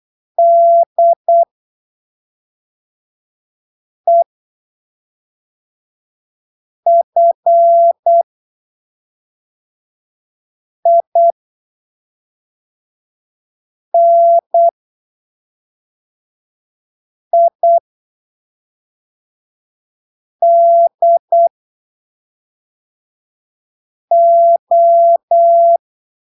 Morse code, heard as a keyed tone, about 8 wpm, spaced as for 3 wpm, 680 Hz. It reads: DEFINIDO
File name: arquivos / texto / texto 01 08ppm